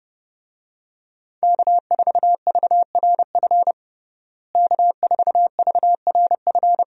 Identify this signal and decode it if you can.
Morse code, heard as a keyed tone, about 30 wpm, 700 Hz. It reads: K4VRF K4VRF
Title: USA Call Signs - 5 Characters - Repeated 2x - 30wpm